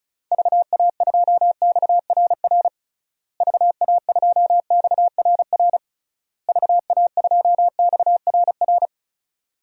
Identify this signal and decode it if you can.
Morse code, heard as a keyed tone, about 34 wpm, 700 Hz. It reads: VA2XRR VA2XRR VA2XRR
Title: DXCC-100 Call Signs - 6 Characters - Repeated 3x - 35wpm